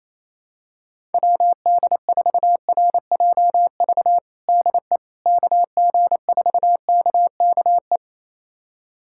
Morse code, heard as a keyed tone, about 28 wpm, 700 Hz. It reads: WD4RJV DE KG4KKE
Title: USA Callsign DE Callsign - 6 Characters - 28wpm